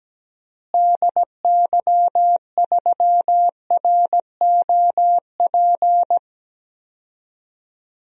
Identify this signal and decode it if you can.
Morse code, heard as a keyed tone, about 17 wpm, 700 Hz. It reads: DY3ROP
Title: DXCC-100 Call Signs - 6 Characters - Repeated 2x - 17wpm